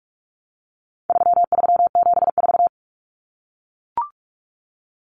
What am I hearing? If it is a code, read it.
Morse code, heard as a keyed tone, about 44 wpm, 700 Hz. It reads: 3374
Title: Numbers - 4-Digit 45wpm